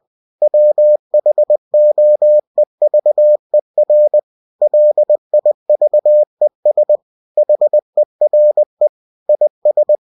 Morse code, heard as a keyed tone, about 20 wpm, 600 Hz. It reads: WHOEVER LIVES HERE IS
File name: cw-books / winnie-the-pooh / winnie-20wpm-page-0048